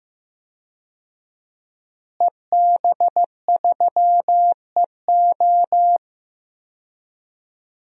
Morse code, heard as a keyed tone, about 15 wpm, 700 Hz. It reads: EB3EO
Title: DXCC-100 Call Signs - 5 Characters - Repeated 2x - 15wpm